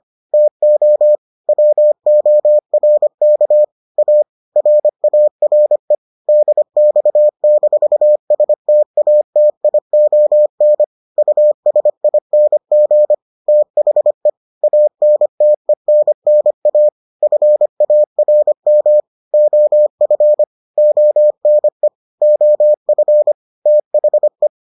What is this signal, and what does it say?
Morse code, heard as a keyed tone, about 25 wpm, 600 Hz. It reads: TO WORK A RARE DX-STATION USING THE ANTENNA FARM OF ONE OF THE